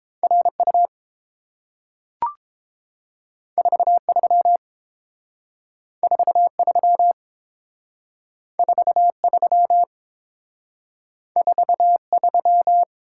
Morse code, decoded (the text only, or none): RU 43 43 43 43